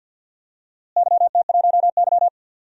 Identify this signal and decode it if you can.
Morse code, heard as a keyed tone, about 49 wpm, 700 Hz. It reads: YT1Y
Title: DXCC-100 Call Signs - 4 Characters - Repeated 2x - 50wpm